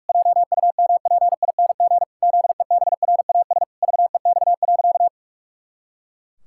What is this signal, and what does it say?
Morse code, heard as a keyed tone, about 46 wpm, 700 Hz. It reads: JUMPING ZEBRAS VEX.
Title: Pangram Challenge v2 - 45wpm